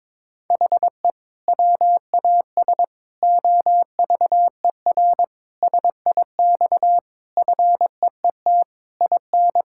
Morse code, heard as a keyed tone, about 22 wpm, 700 Hz. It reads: HE WAS OVER SIX FEET IN
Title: Sentences from Top 1600 Words 22wpm